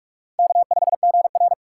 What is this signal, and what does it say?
Morse code, heard as a keyed tone, about 46 wpm, 700 Hz. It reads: K5GR
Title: USA Call Signs - 4 Characters - 4x Character Spacing - 45wpm